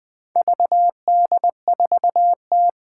Morse code, heard as a keyed tone, about 20 wpm, 700 Hz. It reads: VD4T
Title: Four Challenging Letter-Number Combinations 20wpm